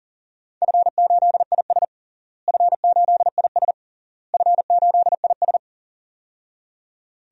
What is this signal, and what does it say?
Morse code, heard as a keyed tone, about 40 wpm, 700 Hz. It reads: F8IS F8IS F8IS
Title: DXCC-100 Call Signs - 4 Characters - Repeated 3x - 40wpm